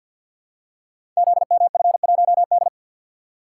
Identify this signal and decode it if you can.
Morse code, heard as a keyed tone, about 49 wpm, 700 Hz. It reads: ZMF1D
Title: Code Groups - 5 Letter:Numbers - 50wpm